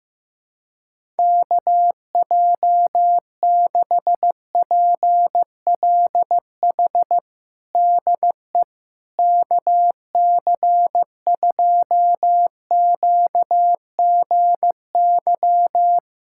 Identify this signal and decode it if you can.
Morse code, heard as a keyed tone, about 15 wpm, 700 Hz. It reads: KJ6PLH DE KC2QGY
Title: USA Callsign DE Callsign - 6 Characters - 15wpm